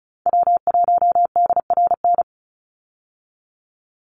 Morse code, heard as a keyed tone, about 34 wpm, 700 Hz. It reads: W1DRN
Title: USA Call Signs 35wpm